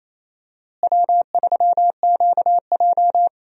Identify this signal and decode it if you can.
Morse code, heard as a keyed tone, about 28 wpm, 700 Hz. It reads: W3QJ
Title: USA Call Signs - 4 Characters - Repeated 3x - 28wpm